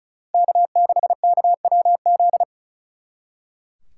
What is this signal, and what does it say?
Morse code, heard as a keyed tone, about 46 wpm, 700 Hz. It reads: K6KWZ